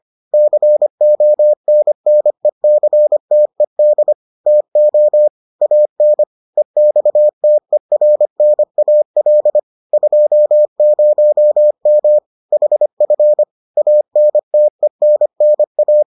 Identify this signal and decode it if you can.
Morse code, heard as a keyed tone, about 25 wpm, 600 Hz. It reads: CONNECTED TO AN EXTERNAL 20M HF ANTENNA